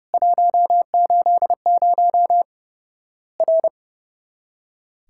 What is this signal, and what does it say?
Morse code, heard as a keyed tone, about 30 wpm, 700 Hz. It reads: 180 R
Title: Worked All Europe DX Contest - QTC Exchange - 30wpm